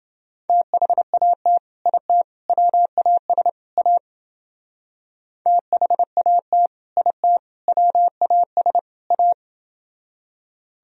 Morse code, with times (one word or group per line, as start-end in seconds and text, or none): 0.5-1.6 THAT
1.9-2.2 IT
2.5-3.5 WAS
3.8-4.0 A
5.5-6.7 THAT
7.0-7.4 IT
7.7-8.8 WAS
9.1-9.3 A